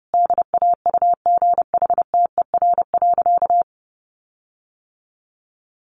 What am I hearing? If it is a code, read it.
Morse code, heard as a keyed tone, about 30 wpm, 700 Hz. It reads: DAUGHTER.
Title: Sentences from Top 1500 Words 30wpm